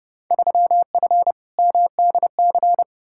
Morse code, heard as a keyed tone, about 32 wpm, 700 Hz. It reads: 3F MDC
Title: Field Day Exchanges Encore - 30wpm